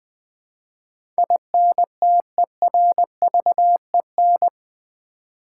Morse code, heard as a keyed tone, about 20 wpm, 700 Hz. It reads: INTERVEN
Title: Build A Word - 4 Letters At A Time - 12-Letter Words - 20wpm